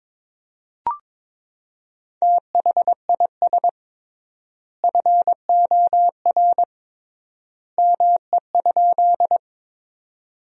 Morse code, heard as a keyed tone, about 21 wpm, 700 Hz. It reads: THIS FOR ME?
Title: 3x Word-Spacing - Sentences from Top 200 Words - 22wpm